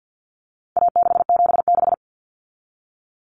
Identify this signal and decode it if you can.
Morse code, heard as a keyed tone, about 50 wpm, 700 Hz. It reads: A676
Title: Common Cut Numbers - 4-Digit 50wpm